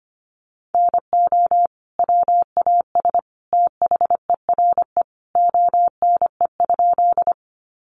Morse code, heard as a keyed tone, about 25 wpm, 700 Hz. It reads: NO WAS THERE ONE?
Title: Sentences from Top 100 Words 25wpm